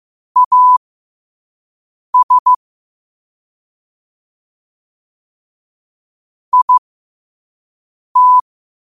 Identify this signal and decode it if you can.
Morse code, heard as a keyed tone, about 15 wpm, spaced as for 5 wpm, 1000 Hz. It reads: AS IT
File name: qotdfiles / 5wpm